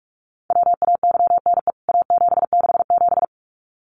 Morse code, heard as a keyed tone, about 44 wpm, 700 Hz. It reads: WAYNE A767
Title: CWT Exchange 45wpm